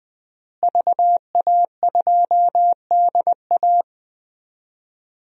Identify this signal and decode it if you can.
Morse code, heard as a keyed tone, about 20 wpm, 700 Hz. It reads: VA2DA
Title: DXCC-100 Call Signs - 5 Characters - Repeated 3x - 20wpm